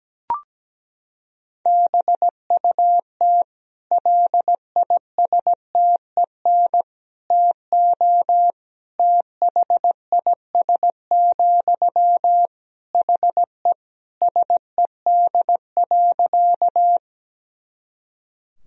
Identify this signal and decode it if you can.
Morse code, heard as a keyed tone, about 17 wpm, 700 Hz. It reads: BUT LISTEN TO THIS, HE SED.
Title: Sentences from Top 400 Words and Abbreviations 17wpm